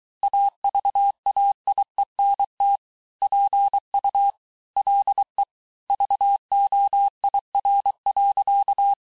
Morse code, heard as a keyed tone, about 23 wpm, 800 Hz. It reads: AVAIENT PU LE VOIR.
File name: cours-du-mercredi / audios / COURS-855--texte-clair-vitesse-22-mpm